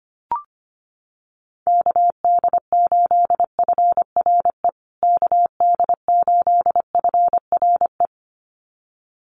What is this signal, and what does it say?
Morse code, heard as a keyed tone, about 25 wpm, 700 Hz. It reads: KD8FRE KD8FRE
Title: USA Call Signs Repeated 2x 25wpm